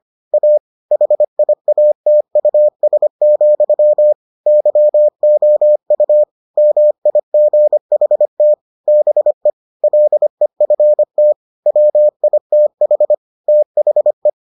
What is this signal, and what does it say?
Morse code, heard as a keyed tone, about 25 wpm, 600 Hz. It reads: A HIATUS, YOU MIGHT BE LEFT WITH THE